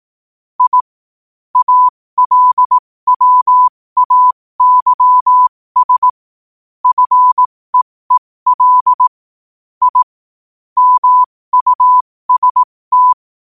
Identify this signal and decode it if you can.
Morse code, heard as a keyed tone, about 18 wpm, spaced as for 15 wpm, 1000 Hz. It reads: I ALWAYS FEEL I MUST
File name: qotdfiles / 15wpm